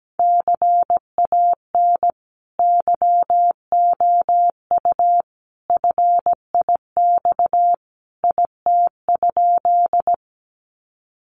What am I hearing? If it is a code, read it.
Morse code, heard as a keyed tone, about 17 wpm, 700 Hz. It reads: CAN YOU FIX IT?
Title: Sentences from Top 1400 Words 17wpm